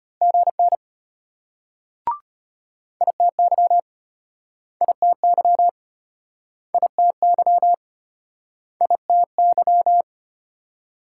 Morse code, heard as a keyed tone, about 35 wpm, 700 Hz. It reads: GN ITY ITY ITY ITY